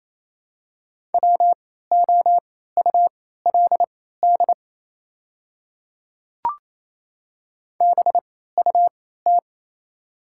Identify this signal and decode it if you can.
Morse code, heard as a keyed tone, about 28 wpm, spaced as for 16 wpm, 700 Hz. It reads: WOULD BUT
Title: 3x Character Spacing - Top 200 Words - 28wpm